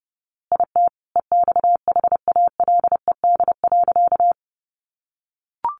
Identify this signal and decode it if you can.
Morse code, heard as a keyed tone, about 30 wpm, 700 Hz. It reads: IT EXHALED.
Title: A Sound Of Thunder 30wpm